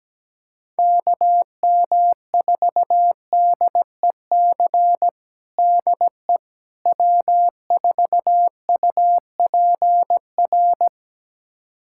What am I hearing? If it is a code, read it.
Morse code, heard as a keyed tone, about 17 wpm, 700 Hz. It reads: KM4DEC DE W4UPR
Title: USA Callsign DE Callsign 17wpm